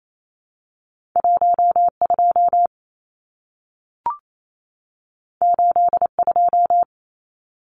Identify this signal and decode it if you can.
Morse code, heard as a keyed tone, about 28 wpm, 700 Hz. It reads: 12 82
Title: Numbers - 2-Digit 28wpm